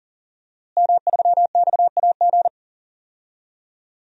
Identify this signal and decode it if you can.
Morse code, heard as a keyed tone, about 40 wpm, 700 Hz. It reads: M3XAG
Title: DXCC-100 Call Signs - 5 Characters - Repeated 3x - 40wpm